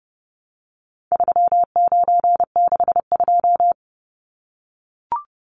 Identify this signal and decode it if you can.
Morse code, heard as a keyed tone, about 30 wpm, 700 Hz. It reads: 3962
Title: Numbers - 4-Digit 30wpm